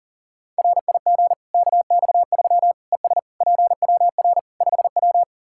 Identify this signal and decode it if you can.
Morse code, heard as a keyed tone, about 40 wpm, 700 Hz. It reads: RIG KX3 ES PWR 5W
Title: Practice QSO - Exchange 2 - Info - 40wpm